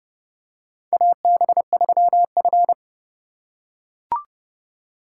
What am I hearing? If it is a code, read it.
Morse code, heard as a keyed tone, about 30 wpm, 700 Hz. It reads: AB3F
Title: USA Call Signs - 4 Characters - 3x Character Spacing - 30wpm